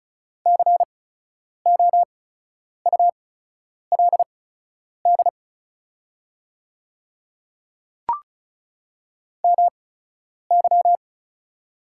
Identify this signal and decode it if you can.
Morse code, heard as a keyed tone, about 36 wpm, spaced as for 10 wpm, 700 Hz. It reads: COULD MY
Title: 8x Character Spacing - Top 100 Words - 35wpm